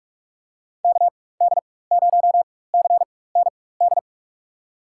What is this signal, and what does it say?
Morse code, heard as a keyed tone, about 46 wpm, spaced as for 21 wpm, 700 Hz. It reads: KD0CND
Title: USA Call Signs - 4x Character Spacing - 45wpm